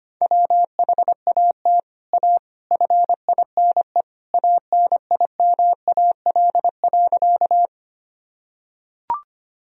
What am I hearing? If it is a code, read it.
Morse code, heard as a keyed tone, about 25 wpm, 700 Hz. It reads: WHAT A FINE ANIMAL.
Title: Sentences from Top 1000 Words - Review Part 1 - 25wpm